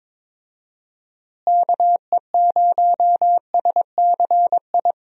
Morse code, heard as a keyed tone, about 22 wpm, 700 Hz. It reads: KE0SCI
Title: USA Call Signs - 6 Characters - Repeated 2x - 22wpm